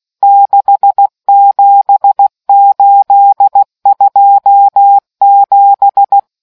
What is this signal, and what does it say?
Morse code, heard as a keyed tone, about 16 wpm, 800 Hz. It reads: 67827